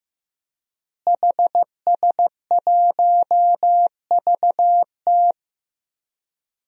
Morse code, heard as a keyed tone, about 15 wpm, 700 Hz. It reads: HS1VT
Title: DXCC-100 Call Signs - 5 Characters - Repeated 3x - 15wpm